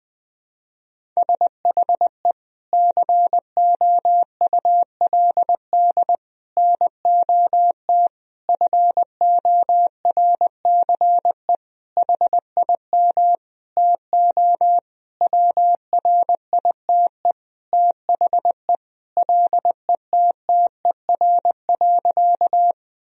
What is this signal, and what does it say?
Morse code, heard as a keyed tone, about 21 wpm, 700 Hz. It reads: SHE COULD NOT FORCE HIM TO WRITE THE LETTER.
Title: Sentences from Top 1000 Words - Review Part 3 - 20wpm